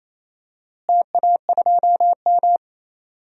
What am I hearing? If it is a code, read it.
Morse code, heard as a keyed tone, about 28 wpm, 700 Hz. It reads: TA2M
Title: DXCC-100 Call Signs - 4 Characters - Repeated 3x - 28wpm